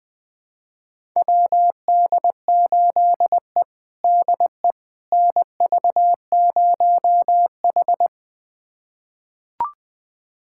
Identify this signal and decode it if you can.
Morse code, heard as a keyed tone, about 21 wpm, 700 Hz. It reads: WD8E DE NV0H
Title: USA Callsign DE Callsign - 4 Characters - 20wpm